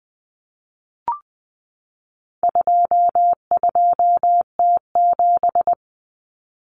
Morse code, heard as a keyed tone, about 20 wpm, 700 Hz. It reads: 22T7
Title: Common Cut Numbers - 4-Digit 20wpm